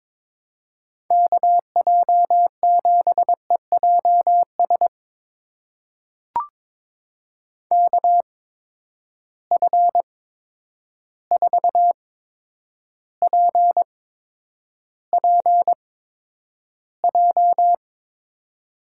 Morse code, heard as a keyed tone, about 22 wpm, spaced as for 9 wpm, 700 Hz. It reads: KJ7EJS KF4PPJ
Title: USA Call Signs - 6 Characters - 8x Character Spacing - 22wpm